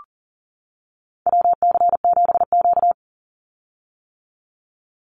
Morse code, heard as a keyed tone, about 40 wpm, 700 Hz. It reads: WC7Q
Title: Call Signs - 40wpm